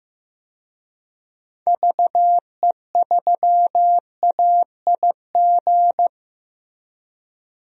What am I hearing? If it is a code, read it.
Morse code, heard as a keyed tone, about 15 wpm, 700 Hz. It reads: VE3AIG
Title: DXCC-100 Call Signs - 6 Characters - Repeated 3x - 15wpm